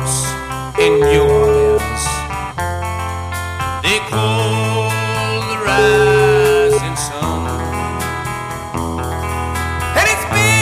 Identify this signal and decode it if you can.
Ringback tone (Europe, 425 Hz), with noise.